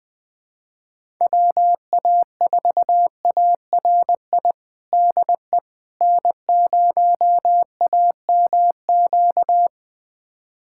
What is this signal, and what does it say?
Morse code, heard as a keyed tone, about 20 wpm, 700 Hz. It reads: WA4ARI DE N0AMQ
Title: USA Callsign DE Callsign 20wpm